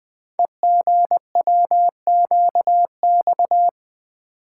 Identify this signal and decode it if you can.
Morse code, heard as a keyed tone, about 20 wpm, 700 Hz. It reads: EGWQX
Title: Code Groups - 5 Letters - 20wpm